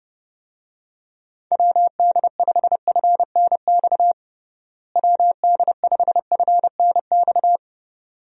Morse code, heard as a keyed tone, about 32 wpm, 700 Hz. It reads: WD5FNX WD5FNX
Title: USA Call Signs - 6 Characters - Repeated 2x - 30wpm